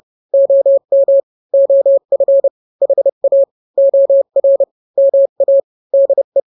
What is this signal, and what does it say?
Morse code, heard as a keyed tone, about 30 wpm, 550 Hz. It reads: OM OF HA OR MA DE
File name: n-grams / 2-grams 30 25 WPM